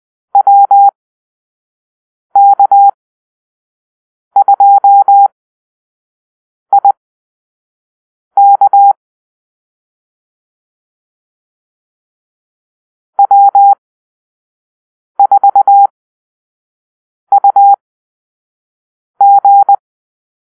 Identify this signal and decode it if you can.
Morse code, heard as a keyed tone, about 20 wpm, spaced as for 5 wpm, 800 Hz. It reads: WK2IK W4UG
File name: arrl / Call-Sign Drill-1